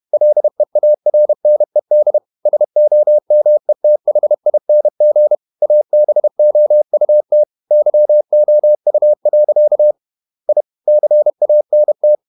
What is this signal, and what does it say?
Morse code, heard as a keyed tone, about 31 wpm, 600 Hz. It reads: LEARNED SOMETHING ABOUT YOU. I CANT
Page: Never 31wpm | CW med Gnister